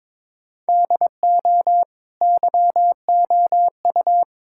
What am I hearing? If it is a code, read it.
Morse code, heard as a keyed tone, about 22 wpm, 700 Hz. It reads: DO YOU
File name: practice / 22wpm / Sentences with 4-Word Phrases - 22wpm